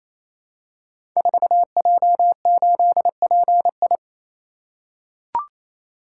Morse code, heard as a keyed tone, about 29 wpm, 700 Hz. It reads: 4J8PI
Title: Code Groups - 5 Letter:Numbers - 28wpm